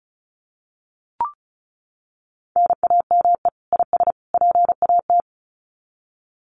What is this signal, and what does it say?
Morse code, heard as a keyed tone, about 34 wpm, 700 Hz. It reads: NAME IS PAT
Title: Common QSO Phrases 35wpm